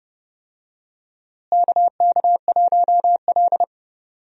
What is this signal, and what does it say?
Morse code, heard as a keyed tone, about 30 wpm, 700 Hz. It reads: KK1L
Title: USA Call Signs - 4 Characters - Repeated 2x - 30wpm